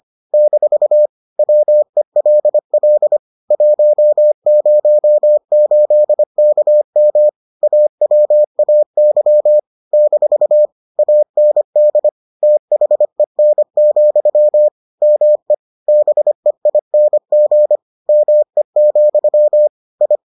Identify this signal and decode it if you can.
Morse code, heard as a keyed tone, about 25 wpm, 600 Hz. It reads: - WELL 108KM AWAY - AND THEN, ME BEING ME, I